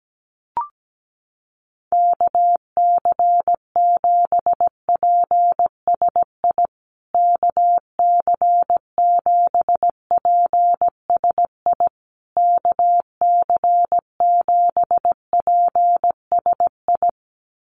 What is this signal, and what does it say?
Morse code, heard as a keyed tone, about 17 wpm, 700 Hz. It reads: KC7PSI KC7PSI KC7PSI
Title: USA Call Signs Repeated 3x 17wpm